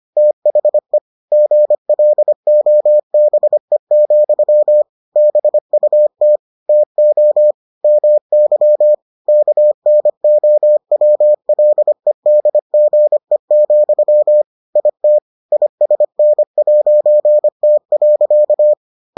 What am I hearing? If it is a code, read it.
Morse code, heard as a keyed tone, about 25 wpm, 600 Hz. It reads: THE GLOBE, BUT TO MY KNOWLEDGE, IT ISN'T.